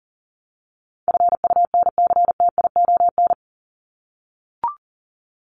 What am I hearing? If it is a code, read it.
Morse code, heard as a keyed tone, about 37 wpm, 700 Hz. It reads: FUNCTION
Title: ES:S Suffix - Top 100 Words with Base Words 40wpm